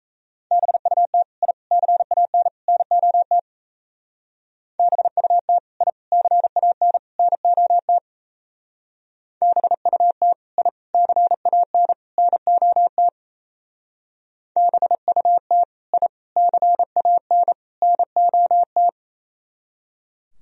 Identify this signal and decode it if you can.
Morse code, heard as a keyed tone, about 36 wpm, 690 Hz. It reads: BUT I CAN NOT BUT I CAN NOT BUT I CAN NOT BUT I CAN NOT